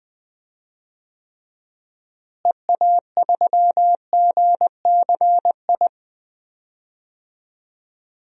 Morse code, heard as a keyed tone, about 20 wpm, 690 Hz. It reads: EA3GCI
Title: DXCC-100 Call Signs - 6 Characters - Repeated 2x - 20wpm